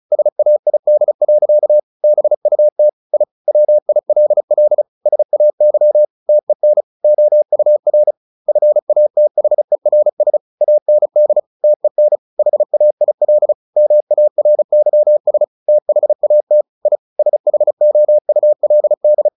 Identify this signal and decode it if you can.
Morse code, heard as a keyed tone, about 34 wpm, 600 Hz. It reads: SAID. BUT I WILL SAY TEN OUR FATHERS AND TEN HAIL MARYS THAT I SHOULD
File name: the old man and the sea / the old man and the sea 1 35wpm